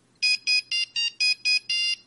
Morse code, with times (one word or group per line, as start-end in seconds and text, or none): none